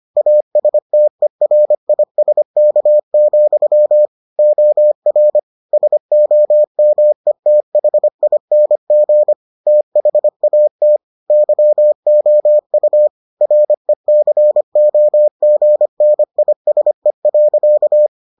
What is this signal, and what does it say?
Morse code, heard as a keyed tone, about 25 wpm, 600 Hz. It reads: ASTERISK, OR SOMETHING THAT YOU RECOGNISE.